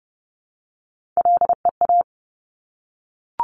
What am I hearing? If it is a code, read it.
Morse code, heard as a keyed tone, about 30 wpm, 700 Hz. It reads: LEA
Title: Top Three Letter Combinations in Words 30wpm